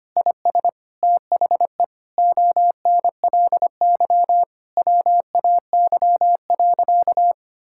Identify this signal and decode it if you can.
Morse code, heard as a keyed tone, about 25 wpm, 700 Hz. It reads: IS THE ONLY WAY.
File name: practice / 25wpm / Sentences with 2-Word Phrases - 25wpm